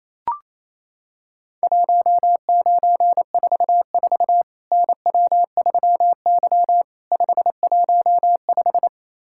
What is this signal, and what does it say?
Morse code, heard as a keyed tone, about 28 wpm, 700 Hz. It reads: 1944 NW3Y 515
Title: Worked All Europe DX Contest - QTC Exchange - 28wpm